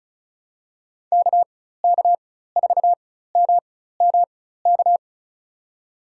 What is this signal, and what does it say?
Morse code, heard as a keyed tone, about 36 wpm, spaced as for 16 wpm, 700 Hz. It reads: KK4MMK
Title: USA Call Signs - 4x Character Spacing - 35wpm